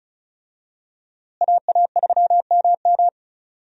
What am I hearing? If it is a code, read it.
Morse code, heard as a keyed tone, about 36 wpm, 700 Hz. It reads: AA3MM
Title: USA Call Signs - 5 Characters - Repeated 3x - 35wpm